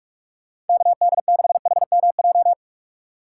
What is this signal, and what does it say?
Morse code, heard as a keyed tone, about 46 wpm, 700 Hz. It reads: KD6HMJ
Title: USA Call Signs - 6 Characters - 4x Character Spacing - 45wpm